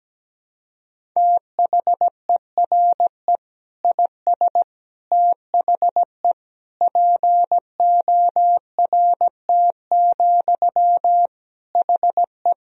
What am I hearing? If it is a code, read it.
Morse code, heard as a keyed tone, about 17 wpm, 700 Hz. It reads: THERE IS THE PORT, HE
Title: Sentences from Top 1000 Words - Review Part 1 - 17wpm